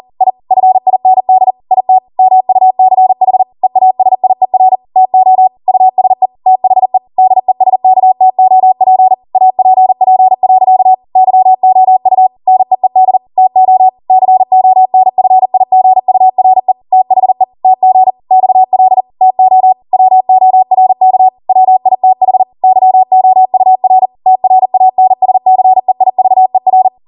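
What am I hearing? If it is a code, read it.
Morse code, heard as a keyed tone, about 40 wpm, 750 Hz. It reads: I FIND IT MUCH EASIER TO USE THE DESKTOP APP. YOU NEED TO CONFIGURE THE TG XL TO WORK WITH YOUR TRANSCEIVER
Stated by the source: Code practice at 40 wpm.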